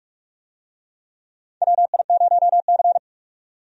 Morse code, heard as a keyed tone, about 46 wpm, 700 Hz. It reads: WI0C
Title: USA Call Signs - 4 Characters - 45wpm